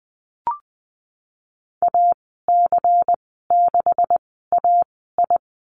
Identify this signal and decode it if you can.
Morse code, heard as a keyed tone, about 20 wpm, spaced as for 15 wpm, 700 Hz. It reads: AC6AI
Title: 140 Lesson 23g - Letter f - Call Signs 20wpm